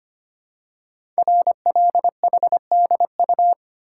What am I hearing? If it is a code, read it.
Morse code, heard as a keyed tone, about 25 wpm, 700 Hz. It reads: RLHDU
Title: Code Groups - 5 Letters - 25wpm